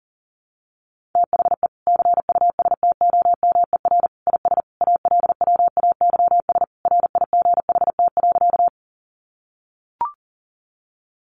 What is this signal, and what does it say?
Morse code, heard as a keyed tone, about 40 wpm, 700 Hz. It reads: THE CUSTOMER IS ALWAYS RIGHT.
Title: Sentences from Top 1100 Words 40wpm